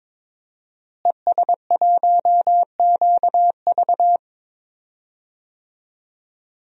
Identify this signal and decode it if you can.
Morse code, heard as a keyed tone, about 22 wpm, 700 Hz. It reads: ES1QV
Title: DXCC-100 Call Signs - 5 Characters - Repeated 3x - 22wpm